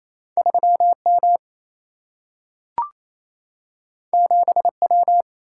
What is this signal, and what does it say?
Morse code, heard as a keyed tone, about 28 wpm, 700 Hz. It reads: 3M 7W
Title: Single-Letter Single-Number 28wpm